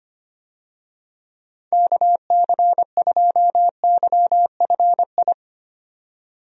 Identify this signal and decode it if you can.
Morse code, heard as a keyed tone, about 25 wpm, 700 Hz. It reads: KC2YFI
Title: USA Call Signs - 6 Characters - 25wpm